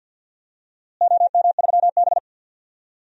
Morse code, heard as a keyed tone, about 50 wpm, 700 Hz. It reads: OM3B
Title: DXCC-100 Call Signs - 4 Characters - Repeated 3x - 50wpm